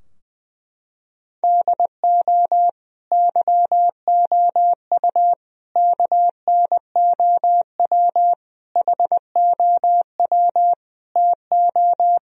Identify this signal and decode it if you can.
Morse code, heard as a keyed tone, about 20 wpm, 700 Hz. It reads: DO YOU KNOW HOW TO
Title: Sentences from Top 1000 Words - Review Part 3 - 20wpm